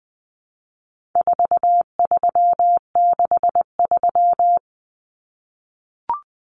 Morse code, heard as a keyed tone, about 20 wpm, 700 Hz. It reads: 4363
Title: Common Cut Numbers - 4-Digit 20wpm